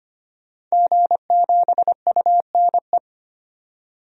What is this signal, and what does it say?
Morse code, heard as a keyed tone, about 25 wpm, 700 Hz. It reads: G7UNE
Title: DXCC-100 Call Signs - 5 Characters - 25wpm